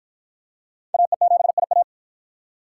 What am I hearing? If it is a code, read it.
Morse code, heard as a keyed tone, about 53 wpm, 700 Hz. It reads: AE7IA